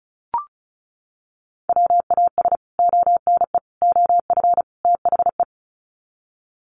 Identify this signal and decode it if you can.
Morse code, heard as a keyed tone, about 34 wpm, 700 Hz. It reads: WAS ONE OF THE
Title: Sets of 4 Words 35wpm